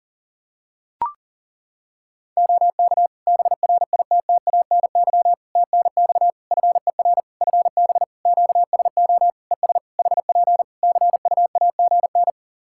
Morse code, heard as a keyed tone, about 40 wpm, 700 Hz. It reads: OK BRITTANY TNX FER FB QSO ES HP CUAGN
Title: K1USN SST QSO Practice 40wpm